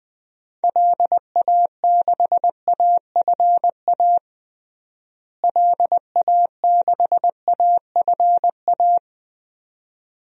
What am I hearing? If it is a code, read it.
Morse code, heard as a keyed tone, about 20 wpm, 700 Hz. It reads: LA6AFA LA6AFA
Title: DXCC-100 Call Signs - 6 Characters - Repeated 2x - 20wpm